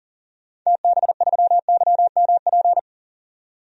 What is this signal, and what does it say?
Morse code, heard as a keyed tone, about 40 wpm, 700 Hz. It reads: TB3YMP
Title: DXCC-100 Call Signs - 6 Characters - 40wpm